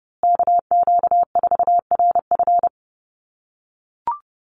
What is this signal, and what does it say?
Morse code, heard as a keyed tone, about 30 wpm, 700 Hz. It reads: KQ4RF
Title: USA Call Signs Repeated 3x 30wpm